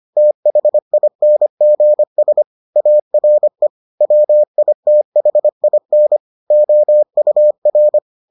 Morse code, heard as a keyed tone, about 25 wpm, 600 Hz. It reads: THINGS ARE WITHIN OUR